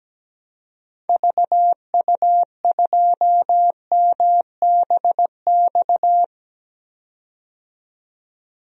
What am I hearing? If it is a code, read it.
Morse code, heard as a keyed tone, about 17 wpm, 700 Hz. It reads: VU2MBX
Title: DXCC-100 Call Signs - 6 Characters - 17wpm